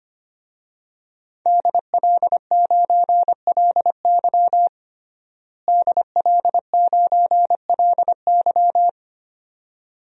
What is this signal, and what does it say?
Morse code, heard as a keyed tone, about 25 wpm, 700 Hz. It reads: DL9LY DL9LY
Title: DXCC-100 Call Signs - 5 Characters - Repeated 2x - 25wpm